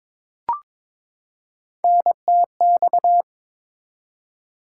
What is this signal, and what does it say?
Morse code, heard as a keyed tone, about 21 wpm, 700 Hz. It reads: NTX
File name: ARRL Sections v2 - 22wpm